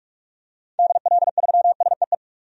Morse code, heard as a keyed tone, about 44 wpm, 700 Hz. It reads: DL3SEE
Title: DXCC-100 Call Signs - 6 Characters - 45wpm